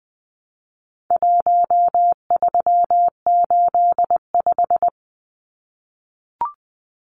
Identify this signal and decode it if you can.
Morse code, heard as a keyed tone, about 19 wpm, 700 Hz. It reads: 1385
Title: Numbers - 4-Digit 20wpm